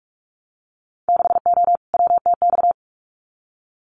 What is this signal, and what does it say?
Morse code, heard as a keyed tone, about 44 wpm, 690 Hz. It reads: 6O WTX
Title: Winter Field Day Exchanges Encore 45wpm